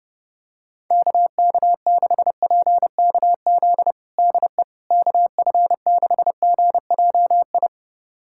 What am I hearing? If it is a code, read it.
Morse code, heard as a keyed tone, about 30 wpm, 700 Hz. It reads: KK6PKZ DE KF6GJI
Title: USA Callsign DE Callsign 30wpm